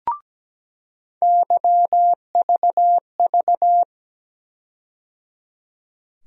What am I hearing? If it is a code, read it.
Morse code, heard as a keyed tone, about 17 wpm, 700 Hz. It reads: YVV
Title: USA Call Sign Postfixes - 3 Letters 17wpm